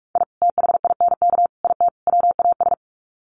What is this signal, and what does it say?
Morse code, heard as a keyed tone, about 44 wpm, 700 Hz. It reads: I THINK IT WAS
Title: Sets of 4 Words 45wpm